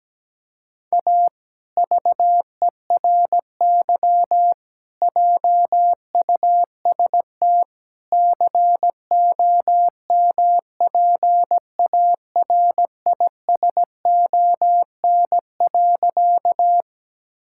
Morse code, heard as a keyed tone, about 17 wpm, 700 Hz. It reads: A VERY JUST COMPARISON.
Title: Sentences from Top 1600 Words 17wpm